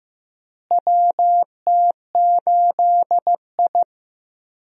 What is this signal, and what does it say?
Morse code, heard as a keyed tone, about 15 wpm, 700 Hz. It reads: WT8I
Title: USA Call Signs - 4 Characters - Repeated 2x - 15wpm